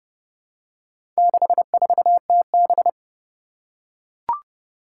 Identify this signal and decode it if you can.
Morse code, heard as a keyed tone, about 30 wpm, 700 Hz. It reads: 64TB
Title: Four Challenging Letter-Number Combinations 30wpm